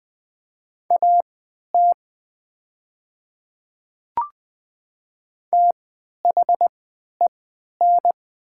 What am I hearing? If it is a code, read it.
Morse code, heard as a keyed tone, about 20 wpm, spaced as for 11 wpm, 700 Hz. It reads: AT THEN
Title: 3x Character Spacing - Top 100 Words - 20wpm